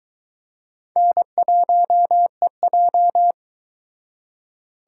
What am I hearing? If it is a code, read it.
Morse code, heard as a keyed tone, about 22 wpm, 700 Hz. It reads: N1EJ